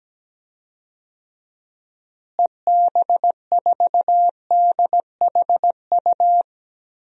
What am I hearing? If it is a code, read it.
Morse code, heard as a keyed tone, about 17 wpm, 700 Hz. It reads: EB4DHU
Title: DXCC-100 Call Signs - 6 Characters - Repeated 2x - 17wpm